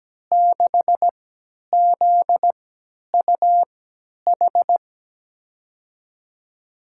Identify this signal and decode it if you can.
Morse code, heard as a keyed tone, about 17 wpm, spaced as for 10 wpm, 700 Hz. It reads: 6ZUH
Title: USA Call Signs - 6 Characters - 3x Character Spacing - 17wpm